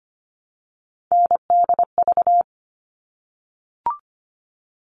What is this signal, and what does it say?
Morse code, heard as a keyed tone, about 25 wpm, 700 Hz. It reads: NDV
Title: All Cut Numbers - 3-Digit 25wpm